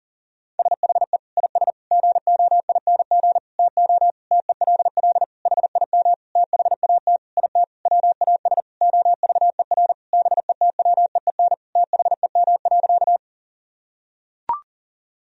Morse code, heard as a keyed tone, about 40 wpm, 700 Hz. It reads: SHE IS GOING TO TELL HIM THAT IT WAS OVER BETWEEN THEM.
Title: Sentences from Top 1000 Words - Review Part 1 - 40wpm